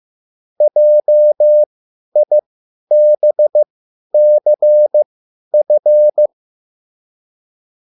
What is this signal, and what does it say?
Morse code, heard as a keyed tone, about 15 wpm, spaced as for 10 wpm, 600 Hz. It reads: JIBCF
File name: QSL / CW lesson